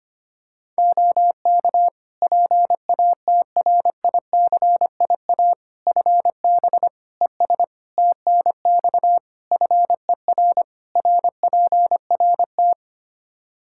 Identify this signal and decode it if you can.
Morse code, heard as a keyed tone, about 25 wpm, 700 Hz. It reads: OK PATRICIA FB ES TNX FER RPRT
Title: Practice QSO - Exchange 2 - Info - 25wpm